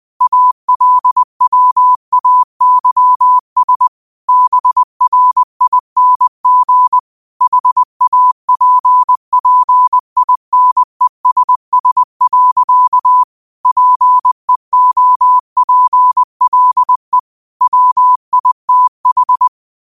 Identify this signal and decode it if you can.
Morse code, heard as a keyed tone, about 20 wpm, 1000 Hz. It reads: ALWAYS BRING HAPPINESS. PEOPLE WITH
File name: qotdfiles / 20wpm